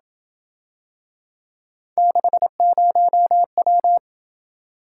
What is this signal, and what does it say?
Morse code, heard as a keyed tone, about 28 wpm, 700 Hz. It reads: 60W